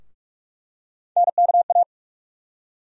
Morse code, heard as a keyed tone, about 47 wpm, 700 Hz. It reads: NKA
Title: USA Call Sign Postfixes - 3 Letters 45wpm